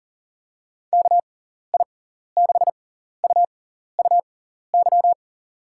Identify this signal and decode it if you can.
Morse code, heard as a keyed tone, about 40 wpm, spaced as for 14 wpm, 700 Hz. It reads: KI6UUY
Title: USA Call Signs - 6 Characters - 6x Character Spacing - 40wpm